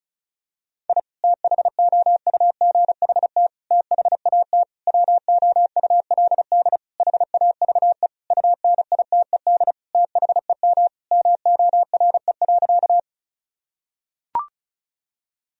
Morse code, heard as a keyed tone, about 36 wpm, 700 Hz. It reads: I THOUGHT THAT WOULD HAVE UNITED THEM MORE.
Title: Sentences from Top 1000 Words - Review Part 4 - 35wpm